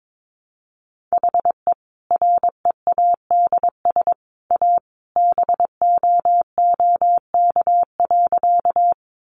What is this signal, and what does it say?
Morse code, heard as a keyed tone, about 22 wpm, 700 Hz. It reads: HE READS A BOOK.
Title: Additional Easy Phrases 22wpm